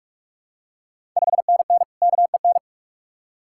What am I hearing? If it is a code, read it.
Morse code, heard as a keyed tone, about 41 wpm, 700 Hz. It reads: 5NN KEN
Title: Ontario QSO Party Exchange 45wpm